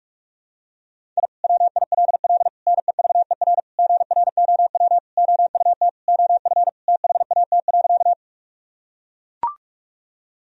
Morse code, heard as a keyed tone, about 44 wpm, 700 Hz. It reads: I WILL NEVER GROW OUT OF THAT.
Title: Sentences from Top 1000 Words - Review Part 1 - 45wpm